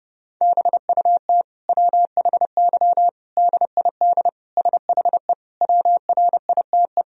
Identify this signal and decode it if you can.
Morse code, heard as a keyed tone, about 30 wpm, 700 Hz. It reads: BUT WHY DID SHE WRITE
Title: Sentences from Top 1000 Words - Review Part 1 - 30wpm